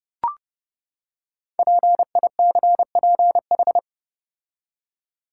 Morse code, heard as a keyed tone, about 30 wpm, 690 Hz. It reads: PICPH
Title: Code Groups - 5 Letters - 30wpm